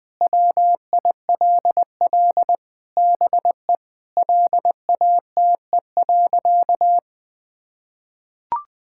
Morse code, heard as a keyed tone, about 21 wpm, 700 Hz. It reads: WILL BE LATE.
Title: Sentences with 4-Word Phrases - 20wpm